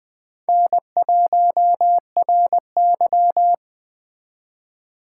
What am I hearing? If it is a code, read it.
Morse code, heard as a keyed tone, about 21 wpm, 700 Hz. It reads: N1RY
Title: Speed Racing - USA 1x2 Call Signs 17wpm